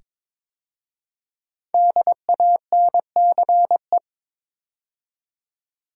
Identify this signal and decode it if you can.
Morse code, heard as a keyed tone, about 22 wpm, 700 Hz. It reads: DANCE
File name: 5 Letter Words - First 3 Letters Given - 22wpm